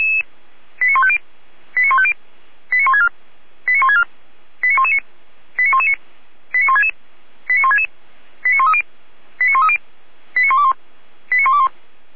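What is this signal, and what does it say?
Five-tone selective call (ZVEI), heard as quick runs of five tones, tones of about 70 ms. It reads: 89150 89150 89164 89164 89119 89119 89166 89166 89130 89130 89121 89121
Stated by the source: ZVEI five-tone selective call received on SDR.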